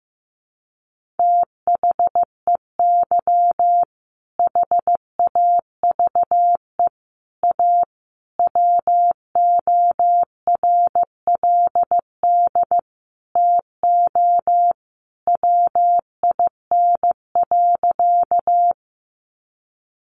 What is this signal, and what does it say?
Morse code, heard as a keyed tone, about 15 wpm, 700 Hz. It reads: THEY HAVE A WORLD TO WIN.
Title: Sentences from Top 1500 Words 15wpm